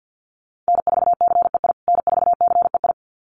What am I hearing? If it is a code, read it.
Morse code, heard as a keyed tone, about 50 wpm, 700 Hz. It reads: N4CEI N4CEI
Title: USA Call Signs Repeated 2x 50wpm